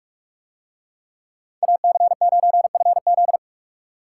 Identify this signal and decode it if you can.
Morse code, heard as a keyed tone, about 47 wpm, 700 Hz. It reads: AC9FZ